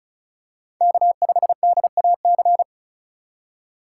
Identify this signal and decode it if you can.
Morse code, heard as a keyed tone, about 36 wpm, 700 Hz. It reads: K5DAC
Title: USA Call Signs - 4x Character Spacing - 35wpm